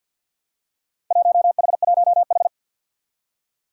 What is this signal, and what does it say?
Morse code, heard as a keyed tone, about 48 wpm, 700 Hz. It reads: 1H1H
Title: Four Challenging Letter-Number Combinations 50wpm